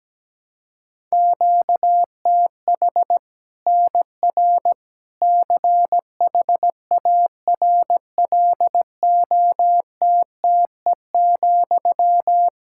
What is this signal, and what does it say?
Morse code, heard as a keyed tone, about 17 wpm, 700 Hz. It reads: QTH NR CHARLOTTE,
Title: Practice QSO - Exchange 1 - Beginning - 17wpm